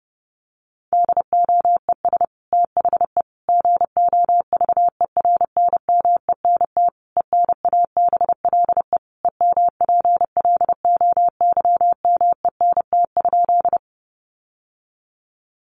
Morse code, heard as a keyed tone, about 30 wpm, 700 Hz. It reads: DOES THE GOVERNMENT ENABLE EMPLOYMENT?
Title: Sentences from Top 1500 Words 30wpm